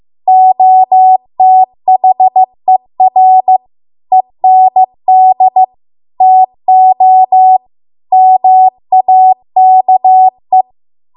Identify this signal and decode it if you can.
Morse code, heard as a keyed tone, about 15 wpm, 750 Hz. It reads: OTHER END TO MAKE